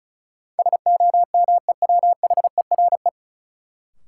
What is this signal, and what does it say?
Morse code, heard as a keyed tone, about 34 wpm, 700 Hz. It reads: SOMEWHERE
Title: Sentences from Top 1400 Words - New Word Drill Alt - 35wpm